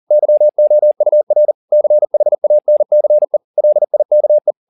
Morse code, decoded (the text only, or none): YOUR CHANCE LIKE